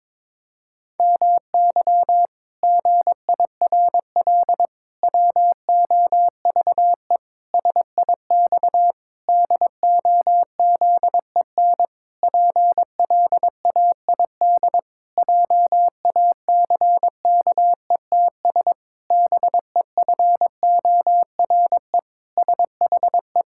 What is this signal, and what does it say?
Morse code, heard as a keyed tone, about 22 wpm, 700 Hz. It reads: MY GIRL WOVE SIX DOZEN PLAID JACKETS BEFORE SHE
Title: Pangram Challenge v2 - 22wpm